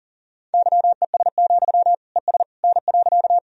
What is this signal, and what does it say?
Morse code, heard as a keyed tone, about 40 wpm, 700 Hz. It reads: YES, ES N.
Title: Sentences from Top 500 Words and Abbreviations 40wpm